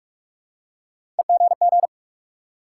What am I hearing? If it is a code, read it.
Morse code, heard as a keyed tone, about 44 wpm, 700 Hz. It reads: EGG
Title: Three Letter Words - 45wpm